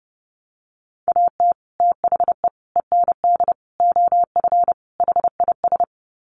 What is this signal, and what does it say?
Morse code, heard as a keyed tone, about 30 wpm, 700 Hz. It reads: AT THE END OF HIS
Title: Sets of 5 Words Encore 30wpm